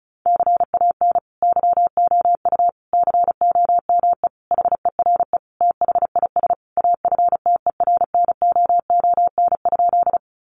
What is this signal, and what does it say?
Morse code, heard as a keyed tone, about 34 wpm, 700 Hz. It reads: CAN YOU COME HERE THIS AFTERNOON?
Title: Sentences from Top 1500 Words 35wpm